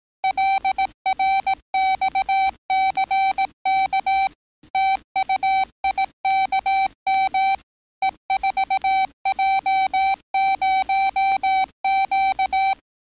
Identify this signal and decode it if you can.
Morse code, heard as a keyed tone, about 18 wpm, 770 Hz. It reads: LRXCK TUIKM E4J0Q
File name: training / NAVPERS 11049to11060 / NAVPERS 11058 RA RB